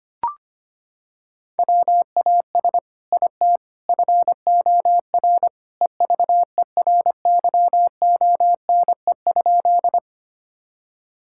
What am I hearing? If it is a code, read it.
Morse code, heard as a keyed tone, about 25 wpm, 700 Hz. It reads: WAS IT FOR EVERYONE?
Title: Sentences from Top 1000 Words - Review Part 5 - 25wpm